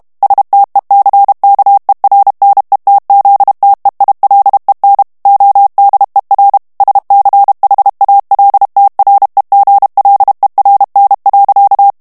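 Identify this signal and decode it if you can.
Morse code, heard as a keyed tone, about 32 wpm, 800 Hz. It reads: STECKERNETZTEILEN ODER SCHALTREGLERN.
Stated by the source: German plain-text Morse reading practice at 30 words per minute.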